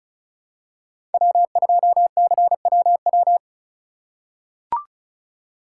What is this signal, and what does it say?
Morse code, heard as a keyed tone, about 36 wpm, 700 Hz. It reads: W2CWW
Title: USA Call Signs - 5 Characters - 8x Character Spacing - 35wpm